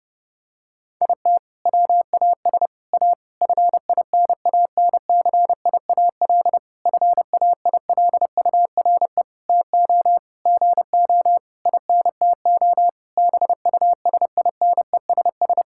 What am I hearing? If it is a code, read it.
Morse code, heard as a keyed tone, about 30 wpm, 700 Hz. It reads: IT WAS A FINANCIAL FAILURE TO GO INTO BUSINESS
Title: Sentences from Top 1000 Words - Review Part 6 - 30wpm